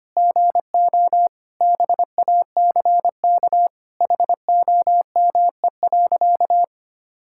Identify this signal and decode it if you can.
Morse code, heard as a keyed tone, about 25 wpm, 700 Hz. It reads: GO BACK HOME.
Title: Sentences from Top 1600 Words 25wpm